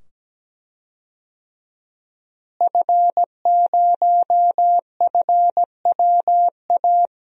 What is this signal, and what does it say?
Morse code, heard as a keyed tone, about 17 wpm, 700 Hz. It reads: F0FWA
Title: DXCC-100 Call Signs - 5 Characters - Repeated 2x - 17wpm